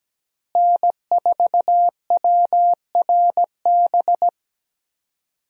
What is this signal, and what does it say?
Morse code, heard as a keyed tone, about 17 wpm, 700 Hz. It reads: N4WRB
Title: USA Call Signs - 5 Characters - Repeated 3x - 17wpm